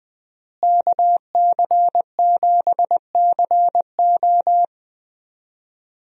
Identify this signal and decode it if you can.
Morse code, heard as a keyed tone, about 20 wpm, 700 Hz. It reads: KC7CO
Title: USA Call Signs - 5 Characters - Repeated 2x - 20wpm